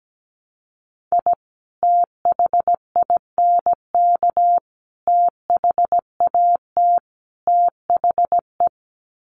Morse code, heard as a keyed tone, about 17 wpm, 700 Hz. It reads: I THINK THAT THE
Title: Sets of 4 Words 17wpm